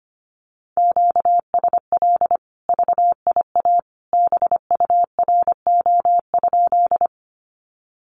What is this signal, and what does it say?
Morse code, heard as a keyed tone, about 25 wpm, 700 Hz. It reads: QSL VIA BURO?
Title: Common QSO Phrases 25wpm